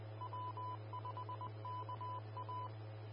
Morse code, heard as a keyed tone, about 20 wpm, 1000 Hz. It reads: W5KA
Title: Austin, TX Amateur Repeaters